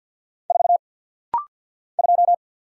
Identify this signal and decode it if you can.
Morse code, heard as a keyed tone, about 48 wpm, 700 Hz. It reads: 4 2
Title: Single Letter-Number Alt - Rapid-Fire 50wpm